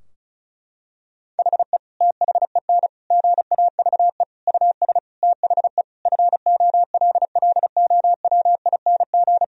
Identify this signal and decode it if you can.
Morse code, heard as a keyed tone, about 36 wpm, 700 Hz. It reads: HE THEN GAVE US THE FOLLOWING
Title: Sentences from Top 1000 Words - Review Part 4 - 35wpm